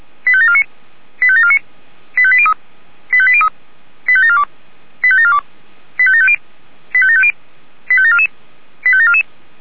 Five-tone selective call (ZVEI), heard as quick runs of five tones, tones of about 70 ms. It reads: 85749 85749 85803 85803 85742 85742 85750 85750 85744 85744